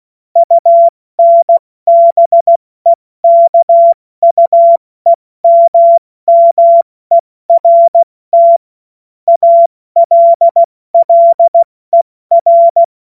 Morse code, heard as a keyed tone, about 16 wpm, 680 Hz. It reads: UNBEKUEMMERT ALLER
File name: Morsekurs / Witze / Witz 18. Januar 2022, 70 F80